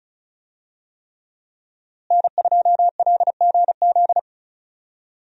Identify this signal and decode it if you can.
Morse code, heard as a keyed tone, about 36 wpm, 700 Hz. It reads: N2LGZ
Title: USA Call Signs - 5 Characters - Repeated 3x - 35wpm